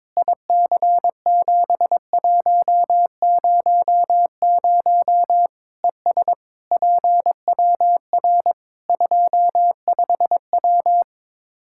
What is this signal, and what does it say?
Morse code, heard as a keyed tone, about 21 wpm, 700 Hz. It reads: IC7100 ES PWR 25W
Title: K1USN SST QSO Practice 22wpm